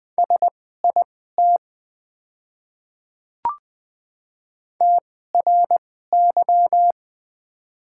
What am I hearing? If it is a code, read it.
Morse code, heard as a keyed tone, about 20 wpm, spaced as for 14 wpm, 700 Hz. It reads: SIT TRY
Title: 2x Character Spacing - Top 500 Words - 20wpm